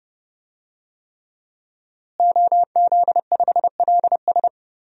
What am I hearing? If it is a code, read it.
Morse code, heard as a keyed tone, about 32 wpm, 700 Hz. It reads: OZ5LS
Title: DXCC-100 Call Signs - 5 Characters - Repeated 3x - 30wpm